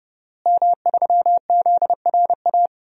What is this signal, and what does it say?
Morse code, heard as a keyed tone, about 30 wpm, 700 Hz. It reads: M3ZRA
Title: DXCC-100 Call Signs - 5 Characters - Repeated 2x - 30wpm